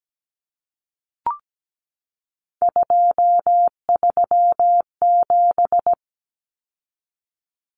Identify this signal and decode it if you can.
Morse code, heard as a keyed tone, about 17 wpm, 700 Hz. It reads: 237
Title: Numbers - 3-Digit 17wpm